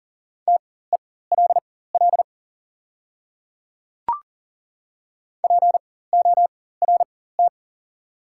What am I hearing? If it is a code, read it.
Morse code, heard as a keyed tone, about 40 wpm, spaced as for 19 wpm, 700 Hz. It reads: TELL PORT
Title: 4x Character Spacing - Top 200 Words - 40wpm